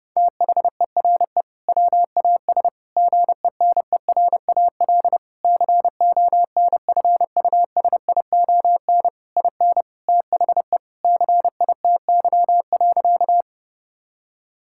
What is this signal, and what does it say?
Morse code, heard as a keyed tone, about 30 wpm, 700 Hz. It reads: THERE WAS GENERAL CONFUSION IN THE CITY.
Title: Sentences from Top 1600 Words 30wpm